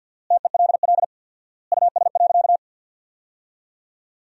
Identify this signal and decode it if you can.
Morse code, heard as a keyed tone, about 50 wpm, 700 Hz. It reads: TELL US.
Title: 4x Word-Spacing - Sentences from Top 400 Words - 50wpm